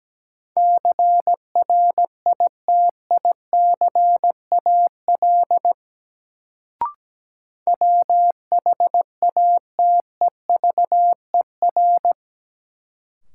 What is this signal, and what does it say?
Morse code, heard as a keyed tone, about 17 wpm, 700 Hz. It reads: CRITICAL WHATEVER
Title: Sentences from Top 1100 Words - New Word Drill - 17wpm